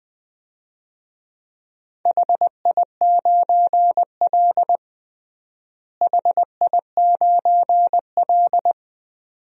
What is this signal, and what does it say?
Morse code, heard as a keyed tone, about 21 wpm, 700 Hz. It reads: HI9L HI9L
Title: DXCC-100 Call Signs - 4 Characters - Repeated 2x - 20wpm